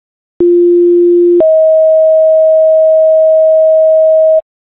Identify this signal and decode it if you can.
Two-tone page, heard as a one-second tone, then a three-second tone, about 350 then 640 Hz. It is heard from 0.4 to 4.4 s.